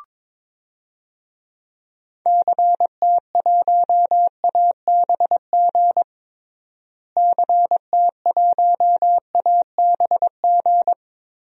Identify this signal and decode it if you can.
Morse code, heard as a keyed tone, about 22 wpm, 700 Hz. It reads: CT1ABG CT1ABG
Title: DXCC-100 Call Signs - 6 Characters - Repeated 2x - 22wpm